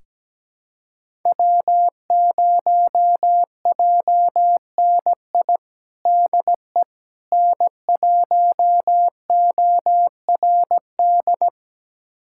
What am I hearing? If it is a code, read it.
Morse code, heard as a keyed tone, about 17 wpm, 700 Hz. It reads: W0JNI DE N1ORD
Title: USA Callsign DE Callsign - 5 Characters - 17wpm